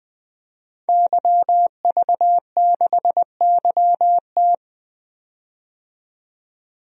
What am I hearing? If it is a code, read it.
Morse code, heard as a keyed tone, about 21 wpm, 700 Hz. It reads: YV6YT
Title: DXCC-100 Call Signs - 5 Characters - 20wpm